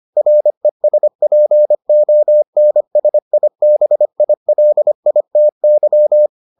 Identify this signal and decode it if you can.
Morse code, heard as a keyed tone, about 25 wpm, 600 Hz. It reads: RESPONSIBILITY